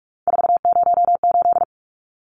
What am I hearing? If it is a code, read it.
Morse code, heard as a keyed tone, about 44 wpm, 700 Hz. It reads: 408
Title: Numbers - 3-Digit 45wpm